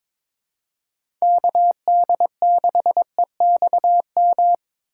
Morse code, heard as a keyed tone, about 22 wpm, 700 Hz. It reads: KD6EXM
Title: USA Call Signs - 4x Character Spacing - 22wpm